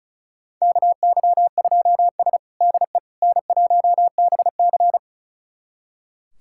Morse code, heard as a keyed tone, about 36 wpm, 700 Hz. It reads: KY2S DE N1BC
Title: USA Callsign DE Callsign - 4 Characters - 35wpm